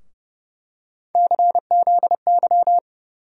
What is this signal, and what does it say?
Morse code, heard as a keyed tone, about 30 wpm, 700 Hz. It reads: CZY
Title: USA Call Sign Postfixes - 3 Letters 30wpm